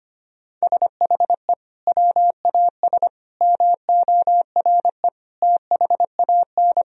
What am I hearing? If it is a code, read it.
Morse code, heard as a keyed tone, about 25 wpm, 700 Hz. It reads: SHE WAS MORE THAN
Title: Sentences from Top 1000 Words - Review Part 3 - 25wpm